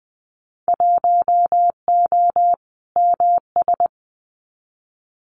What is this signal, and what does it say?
Morse code, heard as a keyed tone, about 20 wpm, 700 Hz. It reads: 1O MS
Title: Winter Field Day Exchanges 20wpm